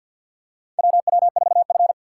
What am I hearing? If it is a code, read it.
Morse code, heard as a keyed tone, about 53 wpm, 690 Hz. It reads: WW4F
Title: USA Call Signs - 4 Characters - 6x Character Spacing - 50wpm